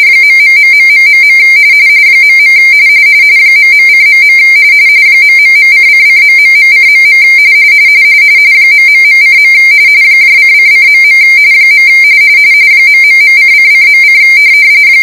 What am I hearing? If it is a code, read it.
RTTY, about 45.45 baud. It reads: ....... .. ..:.. ...:::...:.. ...:. :::.. ...:::.: